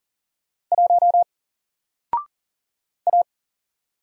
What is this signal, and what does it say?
Morse code, heard as a keyed tone, about 40 wpm, 700 Hz. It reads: 1 A
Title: Single Letter-Number Alt 40wpm